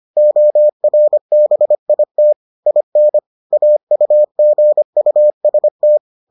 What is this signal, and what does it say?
Morse code, heard as a keyed tone, about 25 wpm, 600 Hz. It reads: ORBIT IN AUGUST